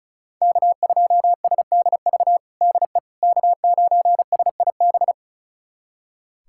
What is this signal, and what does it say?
Morse code, heard as a keyed tone, about 34 wpm, 690 Hz. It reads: K2SDV DE K9SIB
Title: USA Callsign DE Callsign - 5 Characters - 35wpm